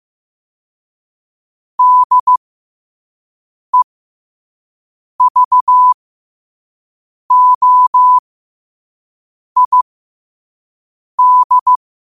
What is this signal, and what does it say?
Morse code, heard as a keyed tone, about 15 wpm, spaced as for 5 wpm, 1000 Hz. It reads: DEVOID
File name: qotdfiles / 5wpm